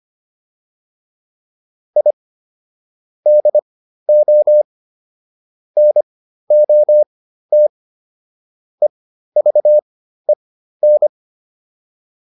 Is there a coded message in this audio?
Yes: Morse code from 2.0 to 11.1 s, heard as a keyed tone, about 25 wpm, spaced as for 13 wpm, 600 Hz. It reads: I DO NOT EVEN